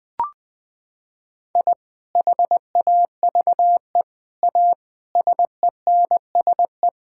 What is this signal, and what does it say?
Morse code, heard as a keyed tone, about 20 wpm, 700 Hz. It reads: I HAVE A SENSE
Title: Sentences with 3-Word Phrases - 20wpm